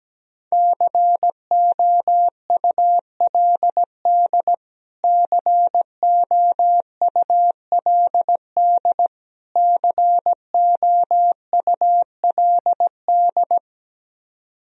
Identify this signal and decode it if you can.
Morse code, heard as a keyed tone, about 17 wpm, 700 Hz. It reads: COULD COULD COULD
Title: Top 100 Words Repeated 17wpm